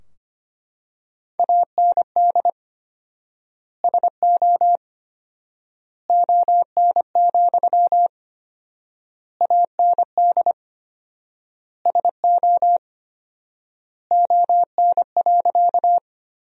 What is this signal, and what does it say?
Morse code, heard as a keyed tone, about 25 wpm, 700 Hz. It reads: AND SO ON, AND SO ON.
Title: 4x Word-Spacing - Sentences from Top 100 Words - 25wpm